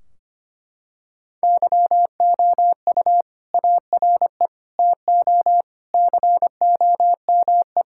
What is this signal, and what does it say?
Morse code, heard as a keyed tone, about 25 wpm, 700 Hz. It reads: YOU ARE TO COME
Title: Sentences from Top 1600 Words 25wpm